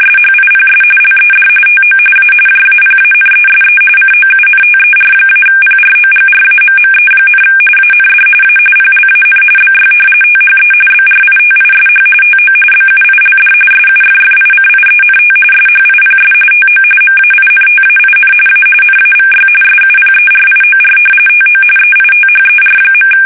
RTTY, about 45.45 baud. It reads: SHIPS LED TO A JAPANESE COUNTER-PROGRAMME, AND A PLAN BY THE BRITISH TO BUILD SUFFICIENT SHIPS TO MAINTAIN A NAVY SUPERIOR TO EITHER.
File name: fsk850